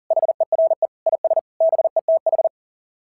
Morse code, heard as a keyed tone, about 37 wpm, 690 Hz. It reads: HERE IS BETH
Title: SOTA QSO Practice 40wpm